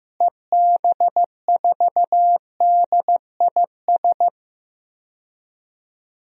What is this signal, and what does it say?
Morse code, heard as a keyed tone, about 16 wpm, 700 Hz. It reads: EB4DIS
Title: DXCC-100 Call Signs - 6 Characters - Repeated 3x - 15wpm